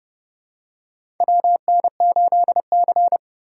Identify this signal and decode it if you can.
Morse code, heard as a keyed tone, about 30 wpm, 700 Hz. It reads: WN8C
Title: USA Call Signs - 6x Character Spacing - 30wpm